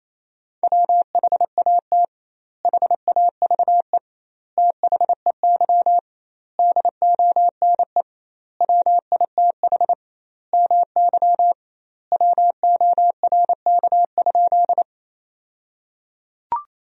Morse code, heard as a keyed tone, about 28 wpm, 700 Hz. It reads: WHAT HAVE THEY DONE WITH MY WORK?
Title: 2x Word-Spacing - Sentences from Top 500 Words - 28wpm